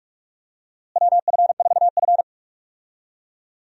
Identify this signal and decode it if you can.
Morse code, heard as a keyed tone, about 46 wpm, 700 Hz. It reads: WF4F
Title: USA Call Signs - 4 Characters - Repeated 3x - 45wpm